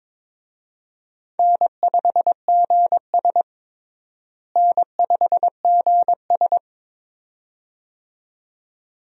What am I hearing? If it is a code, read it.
Morse code, heard as a keyed tone, about 22 wpm, 700 Hz. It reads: N5GS N5GS
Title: USA Call Signs - 4 Characters - Repeated 2x - 22wpm